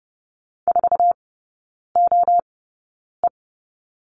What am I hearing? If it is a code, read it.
Morse code, heard as a keyed tone, about 30 wpm, spaced as for 9 wpm, 690 Hz. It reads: 4OE
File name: Lesson 16e - Letter c - Three Characters 30wpm